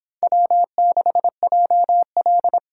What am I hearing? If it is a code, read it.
Morse code, heard as a keyed tone, about 29 wpm, 700 Hz. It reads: W6JL